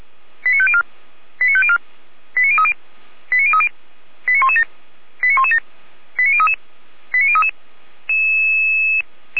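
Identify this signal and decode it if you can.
Five-tone selective call (ZVEI), heard as quick runs of five tones, tones of about 70 ms. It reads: 89504 89504 89030 89030 89117 89117 89033 89033
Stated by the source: Received on an SDR; ZVEI five-tone call.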